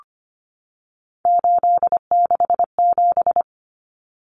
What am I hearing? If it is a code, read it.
Morse code, heard as a keyed tone, about 25 wpm, 700 Hz. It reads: 867
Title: Common Cut Numbers - 3-Digit 25wpm